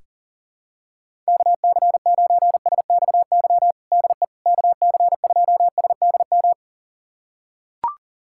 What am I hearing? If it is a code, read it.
Morse code, heard as a keyed tone, about 40 wpm, 700 Hz. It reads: KC9SXY DE KC2SDM
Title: USA Callsign DE Callsign - 6 Characters - 40wpm